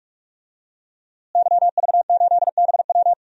Morse code, heard as a keyed tone, about 42 wpm, 700 Hz. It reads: YV8BW
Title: DXCC-100 Call Signs - 5 Characters - Repeated 2x - 45wpm